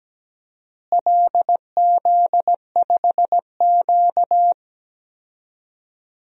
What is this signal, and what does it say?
Morse code, heard as a keyed tone, about 17 wpm, 700 Hz. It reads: LZ5Q
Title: DXCC-100 Call Signs - 4 Characters - Repeated 2x - 17wpm